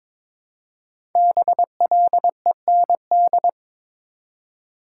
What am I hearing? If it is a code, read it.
Morse code, heard as a keyed tone, about 22 wpm, 700 Hz. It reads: BLEND
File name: 5 Letter Words - First 3 Letters Given - 22wpm